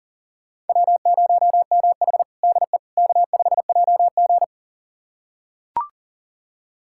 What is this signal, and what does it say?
Morse code, heard as a keyed tone, about 40 wpm, 700 Hz. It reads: W0MH DE K5JG
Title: USA Callsign DE Callsign - 4 Characters - 40wpm